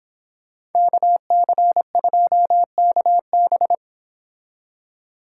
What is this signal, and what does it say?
Morse code, heard as a keyed tone, about 26 wpm, 700 Hz. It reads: KC2KB